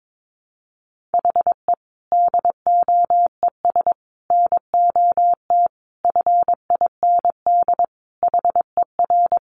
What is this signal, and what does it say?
Morse code, heard as a keyed tone, about 22 wpm, 700 Hz. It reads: HE DOES NOT FIND HER
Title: Easy Phrases 22wpm